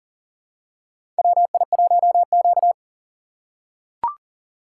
Morse code, heard as a keyed tone, about 40 wpm, 700 Hz. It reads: WI1Q
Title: USA Call Signs - 3x Character Spacing - 40wpm